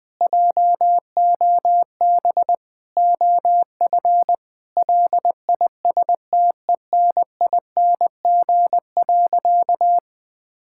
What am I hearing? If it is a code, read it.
Morse code, heard as a keyed tone, about 20 wpm, 700 Hz. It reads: JOB OF LISTENING.
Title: Sentences with 6-Word Phrases - Spoken Beginning - 20wpm